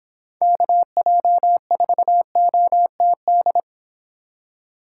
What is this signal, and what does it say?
Morse code, heard as a keyed tone, about 26 wpm, 700 Hz. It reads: KJ4OTD